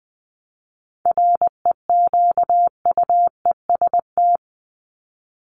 Morse code, heard as a keyed tone, about 20 wpm, 700 Hz. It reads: REQUEST
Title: Top 1751-2000 Words 20wpm